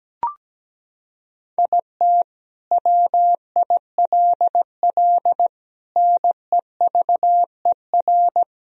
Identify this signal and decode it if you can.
Morse code, heard as a keyed tone, about 17 wpm, 700 Hz. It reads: IT WILL NEVER
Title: Sentences from Top 1000 Words - Review Part 2 - 17wpm